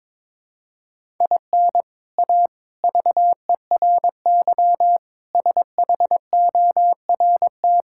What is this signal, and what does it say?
Morse code, heard as a keyed tone, about 22 wpm, 700 Hz. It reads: IN A VERY SHORT
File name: practice / 22wpm / Sets of 7 Words - 22wpm